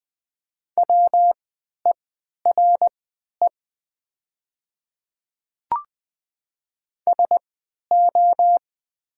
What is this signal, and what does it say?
Morse code, heard as a keyed tone, about 20 wpm, spaced as for 11 wpm, 700 Hz. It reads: WERE SO
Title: 3x Character Spacing - Top 100 Words - 20wpm